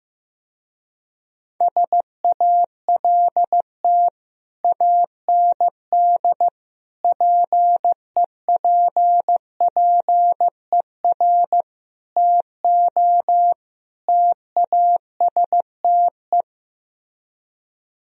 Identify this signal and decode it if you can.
Morse code, heard as a keyed tone, about 15 wpm, 700 Hz. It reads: SALT AND PEPPER TO TASTE
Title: Speed-Racing - Sets of 5 Words 15wpm